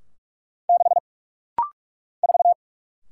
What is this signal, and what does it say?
Morse code, heard as a keyed tone, about 46 wpm, 700 Hz. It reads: 6 4
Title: Single Letter-Number - Rapid-Fire 45wpm